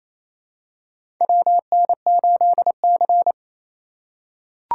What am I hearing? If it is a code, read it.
Morse code, heard as a keyed tone, about 28 wpm, 700 Hz. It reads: WN8C
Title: USA Call Signs - 4 Characters - 4x Character Spacing - 28wpm